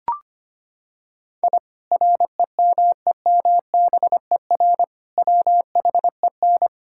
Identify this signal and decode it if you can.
Morse code, heard as a keyed tone, about 25 wpm, 700 Hz. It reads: I REMEMBER WHEN
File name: practice / 25wpm / Sentences with 2-Word Phrases - 25wpm